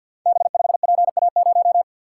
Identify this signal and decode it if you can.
Morse code, heard as a keyed tone, about 51 wpm, 700 Hz. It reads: B5PA0
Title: Code Groups - 5 Letter:Numbers - 50wpm